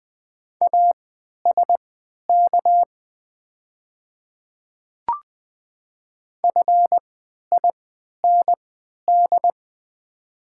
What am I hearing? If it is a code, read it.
Morse code, heard as a keyed tone, about 20 wpm, spaced as for 11 wpm, 700 Hz. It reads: ASK FIND
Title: 3x Character Spacing - Top 200 Words - 20wpm